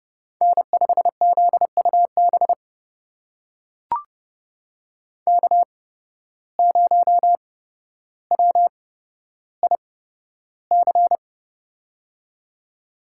Morse code, heard as a keyed tone, about 32 wpm, spaced as for 13 wpm, 700 Hz. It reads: N5ZUB K0WIC
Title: USA Call Signs - 5 Characters - 8x Character Spacing - 30wpm